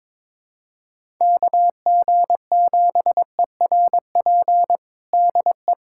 Morse code, heard as a keyed tone, about 21 wpm, 700 Hz. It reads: KG7ERP DE
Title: USA Callsign DE Callsign 22wpm